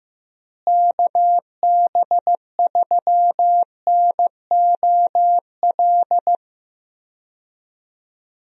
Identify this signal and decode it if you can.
Morse code, heard as a keyed tone, about 15 wpm, 700 Hz. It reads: KB3NOL
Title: USA Call Signs - 6 Characters - Repeated 2x - 15wpm